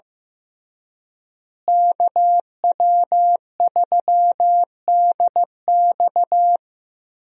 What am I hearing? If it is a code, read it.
Morse code, heard as a keyed tone, about 15 wpm, 700 Hz. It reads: KW3DX
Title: USA Call Signs - 5 Characters - Repeated 3x - 15wpm